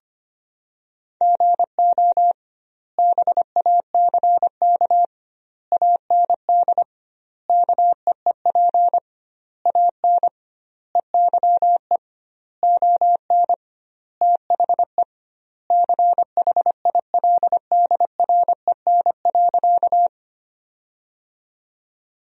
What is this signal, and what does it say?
Morse code, heard as a keyed tone, about 25 wpm, 700 Hz. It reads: GO BACK AND KEEP AN EYE ON THE CHILDREN.
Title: 2x Word-Spacing - Sentences from Top 300 Words - 25wpm